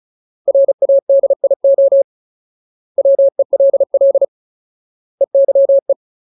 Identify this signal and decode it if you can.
Morse code, heard as a keyed tone, about 34 wpm, 550 Hz. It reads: RADIO WELL EYE
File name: Ultimate Word List / 35WPM Ultimate